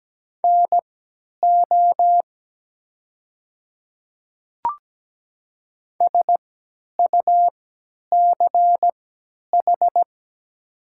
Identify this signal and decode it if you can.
Morse code, heard as a keyed tone, about 17 wpm, spaced as for 10 wpm, 700 Hz. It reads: NO SUCH
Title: 3x Character Spacing - Top 200 Words - 17wpm